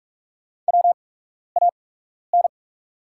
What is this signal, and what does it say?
Morse code, heard as a keyed tone, about 43 wpm, spaced as for 12 wpm, 700 Hz. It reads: WAN
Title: 8x Character Spacing - Top 200 Words - 45wpm